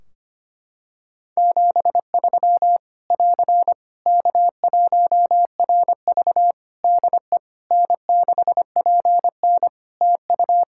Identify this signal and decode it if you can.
Morse code, heard as a keyed tone, about 25 wpm, 700 Hz. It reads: 73 <AR> K1RV DE N6PN TU
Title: Practice QSO - Exchange 4 - Ending - 25wpm